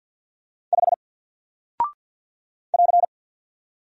Morse code, heard as a keyed tone, about 48 wpm, 700 Hz. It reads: 5 <AR>
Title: Single Character (Letter:Number:Punctuation:Prosign) Alt 50wpm